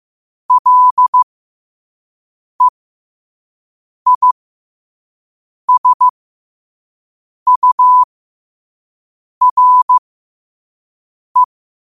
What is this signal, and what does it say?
Morse code, heard as a keyed tone, about 15 wpm, spaced as for 5 wpm, 1000 Hz. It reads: LEISURE
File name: qotdfiles / 5wpm